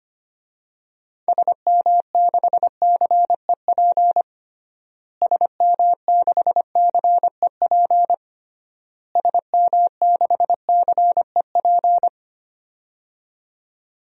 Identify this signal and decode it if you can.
Morse code, heard as a keyed tone, about 25 wpm, 700 Hz. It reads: SM6CEP SM6CEP SM6CEP
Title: DXCC-100 Call Signs - 6 Characters - Repeated 3x - 25wpm